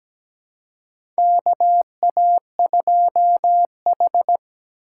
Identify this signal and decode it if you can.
Morse code, heard as a keyed tone, about 17 wpm, 700 Hz. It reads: KA2H
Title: USA Callsign DE Callsign - 4 Characters - 17wpm